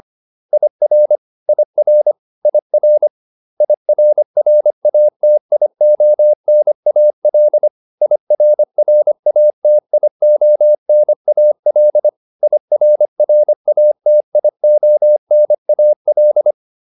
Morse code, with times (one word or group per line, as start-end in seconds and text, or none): 0.5-1.2 IR
1.5-2.1 IR
2.5-3.1 IR
3.6-7.7 IRRATIONAL
8.0-12.1 IRRATIONAL
12.4-16.5 IRRATIONAL